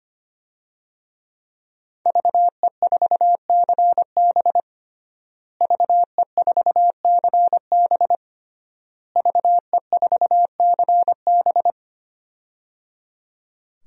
Morse code, heard as a keyed tone, about 25 wpm, 700 Hz. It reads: VE4CB VE4CB VE4CB
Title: DXCC-100 Call Signs - 5 Characters - Repeated 3x - 25wpm